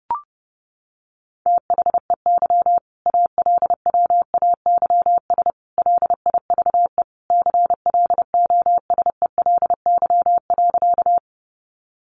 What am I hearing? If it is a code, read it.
Morse code, heard as a keyed tone, about 30 wpm, 700 Hz. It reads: THEY ALWAYS LIVE CLOSELY.
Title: Sentences from Top 1400 Words 30wpm